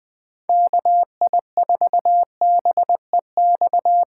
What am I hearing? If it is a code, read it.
Morse code, heard as a keyed tone, about 20 wpm, 700 Hz. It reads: KI4BEX
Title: USA Call Signs - 6 Characters - 3x Character Spacing - 20wpm